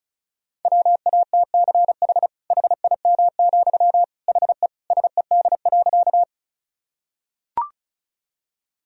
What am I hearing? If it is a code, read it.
Morse code, heard as a keyed tone, about 36 wpm, 700 Hz. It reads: WATCH HIM, HE SED.
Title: Sentences from Top 300 Words and Abbreviations 35wpm